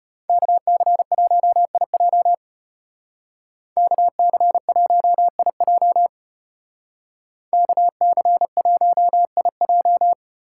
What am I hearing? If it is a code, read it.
Morse code, heard as a keyed tone, about 30 wpm, 700 Hz. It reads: KC1IJ KC1IJ KC1IJ
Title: Speed Racing - USA 2x2 Call Signs 25wpm